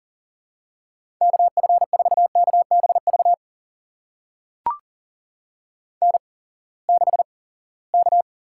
Morse code, heard as a keyed tone, about 42 wpm, spaced as for 23 wpm, 700 Hz. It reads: KF4KBV N6K
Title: USA Call Signs - 8x Character Spacing - 40wpm